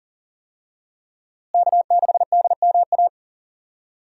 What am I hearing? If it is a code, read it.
Morse code, heard as a keyed tone, about 40 wpm, 700 Hz. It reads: K6DMA
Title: USA Call Signs - 5 Characters - Repeated 2x - 40wpm